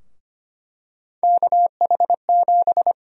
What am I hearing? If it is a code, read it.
Morse code, heard as a keyed tone, about 25 wpm, 700 Hz. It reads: KH7
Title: USA Call Sign Prefixes - 25wpm